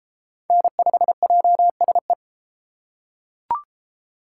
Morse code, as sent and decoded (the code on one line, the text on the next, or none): -. ..... .--- ... .
N5JSE